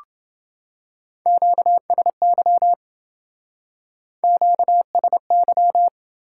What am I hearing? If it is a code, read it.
Morse code, heard as a keyed tone, about 27 wpm, 700 Hz. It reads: QSY QSY